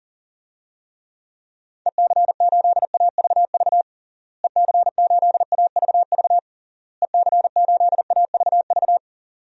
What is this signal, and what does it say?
Morse code, heard as a keyed tone, about 40 wpm, 700 Hz. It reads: EC8AVV EC8AVV EC8AVV
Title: DXCC-100 Call Signs - 6 Characters - Repeated 3x - 40wpm